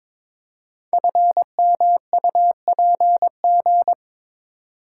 Morse code, heard as a keyed tone, about 22 wpm, 700 Hz. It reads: FMUPG
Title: Code Groups - 5 Letters - 22wpm